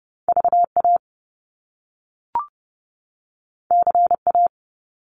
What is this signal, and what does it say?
Morse code, heard as a keyed tone, about 30 wpm, 700 Hz. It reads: VA CA
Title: US States 30wpm